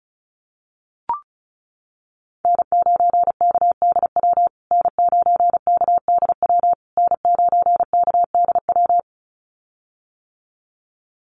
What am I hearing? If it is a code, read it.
Morse code, heard as a keyed tone, about 34 wpm, 700 Hz. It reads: N9KDW N9KDW N9KDW
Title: USA Call Signs Repeated 3x 35wpm